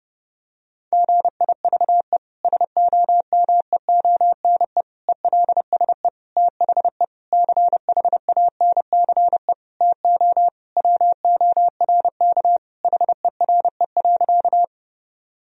Morse code, heard as a keyed tone, about 30 wpm, 700 Hz. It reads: GIVE SOMEONE ELSE THE CHANCE TO WORK HERE.
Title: Sentences from Top 1000 Words - Review Part 4 - 30wpm